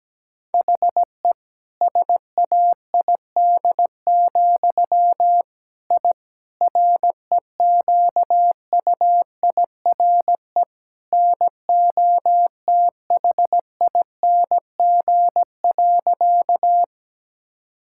Morse code, heard as a keyed tone, about 17 wpm, 700 Hz. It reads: HE SAID, I REQUIRE NOTHING.
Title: Sentences from Top 1000 Words - Review Part 6 - 17wpm